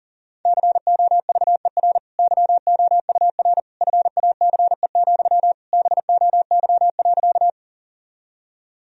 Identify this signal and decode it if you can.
Morse code, heard as a keyed tone, about 40 wpm, 700 Hz. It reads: COVER YOUR FACE, BOY.
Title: Sentences from Top 1000 Words - Review Part 1 - 40wpm